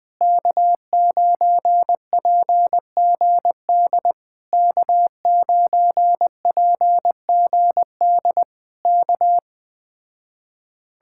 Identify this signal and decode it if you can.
Morse code, heard as a keyed tone, about 20 wpm, 700 Hz. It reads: K9PGD K9PGD K
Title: Speed-Racing Common QSO Phrases - 17wpm